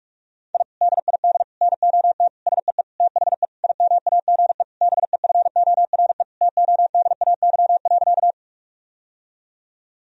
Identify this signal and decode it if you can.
Morse code, heard as a keyed tone, about 41 wpm, 700 Hz. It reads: I DID NOT SEE THE IMAGE BEFORE TODAY.
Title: Sentences from Top 1000 Words - Review Part 4 - 45wpm